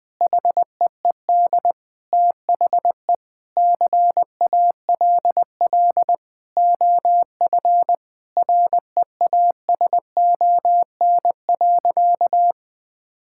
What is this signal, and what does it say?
Morse code, heard as a keyed tone, about 20 wpm, 700 Hz. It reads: HEED THE CALL OF REASON.
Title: Sentences with 8-Word Phrases - 20wpm